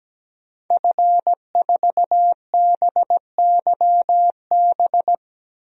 Morse code, heard as a keyed tone, about 17 wpm, 700 Hz. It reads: F4BYB
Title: DXCC-100 Call Signs - 5 Characters - 17wpm